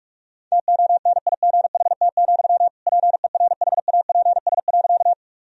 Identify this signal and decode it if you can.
Morse code, heard as a keyed tone, about 46 wpm, 700 Hz. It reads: TONIGHT, PERHAPS.
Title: Sentences from Top 1000 Words - Review Part 3 - 45wpm